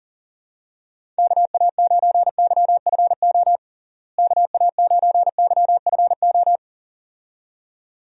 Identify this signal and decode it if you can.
Morse code, heard as a keyed tone, about 40 wpm, 700 Hz. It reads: KA9YFO KA9YFO
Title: USA Call Signs - 6 Characters - Repeated 2x - 40wpm